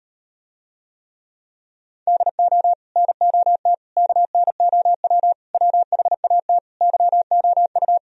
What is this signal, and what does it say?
Morse code, heard as a keyed tone, about 39 wpm, 700 Hz. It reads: DO NOT KNOW WHAT YOU